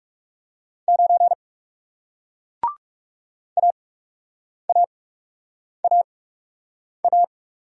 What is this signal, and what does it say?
Morse code, heard as a keyed tone, about 40 wpm, spaced as for 8 wpm, 700 Hz. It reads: 9 AAAA